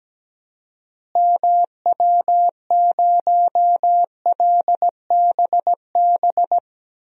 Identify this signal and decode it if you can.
Morse code, heard as a keyed tone, about 17 wpm, 700 Hz. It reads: MW0LBB
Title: DXCC-100 Call Signs - 6 Characters - Repeated 2x - 17wpm